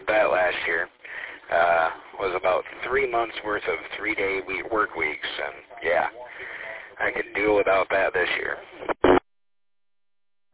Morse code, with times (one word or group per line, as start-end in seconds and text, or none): none